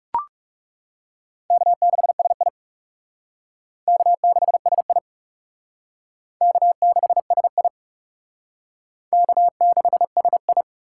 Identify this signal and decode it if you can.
Morse code, heard as a keyed tone, about 35 wpm, 700 Hz. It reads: K6SI K6SI K6SI K6SI
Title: Speed Racing - USA 1x2 Call Signs 30wpm